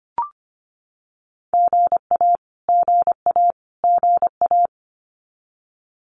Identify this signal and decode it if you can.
Morse code, heard as a keyed tone, about 25 wpm, 700 Hz. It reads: GA GA GA
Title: Abbreviations Repeated 3x - 25wpm